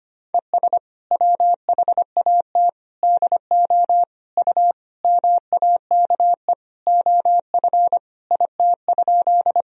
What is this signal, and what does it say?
Morse code, heard as a keyed tone, about 25 wpm, 700 Hz. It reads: ES WHAT DO U MAKE OF IT?
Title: Sentences from Top 100 Words and Abbreviations 25wpm